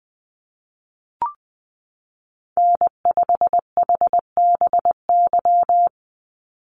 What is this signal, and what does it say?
Morse code, heard as a keyed tone, about 20 wpm, 700 Hz. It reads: N5HBY
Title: USA Call Signs 20wpm